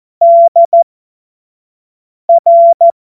Morse code, heard as a keyed tone, about 14 wpm, spaced as for 5 wpm, 680 Hz. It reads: DR
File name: Morsekurs / Witze / Witz 09. Dezember 2025 25 F70